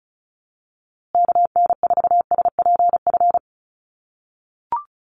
Morse code, heard as a keyed tone, about 34 wpm, 700 Hz. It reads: KN4SPF
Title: USA Call Signs Repeated 3x 35wpm